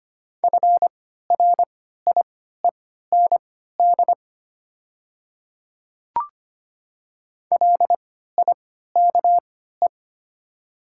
Morse code, heard as a keyed tone, about 25 wpm, spaced as for 14 wpm, 700 Hz. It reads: FRIEND LIKE
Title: 3x Character Spacing - Top 300 Words - 25wpm